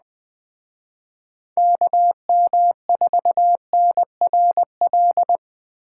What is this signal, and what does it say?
Morse code, heard as a keyed tone, about 20 wpm, 700 Hz. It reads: KM4NRL
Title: Speed Racing - USA 2x3 Call Signs 17wpm